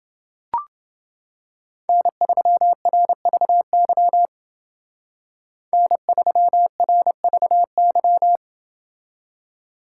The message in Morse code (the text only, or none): N3RVY N3RVY